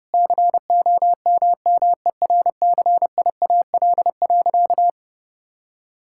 Morse code, heard as a keyed tone, about 32 wpm, 700 Hz. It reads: COMMERCIAL.
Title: Sentences from Top 1000 Words - Review Part 5 - 30wpm